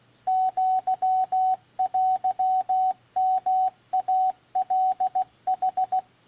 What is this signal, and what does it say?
Morse code (Japanese Wabun code), heard as a keyed tone, about 16 wpm, 750 Hz, with radio noise. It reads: アテヨイカヌ